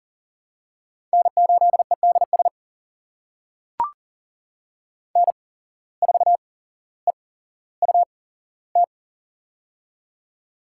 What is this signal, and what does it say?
Morse code, heard as a keyed tone, about 40 wpm, spaced as for 17 wpm, 700 Hz. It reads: N8EDS N4EUT
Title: USA Call Signs - 5 Characters - 8x Character Spacing - 40wpm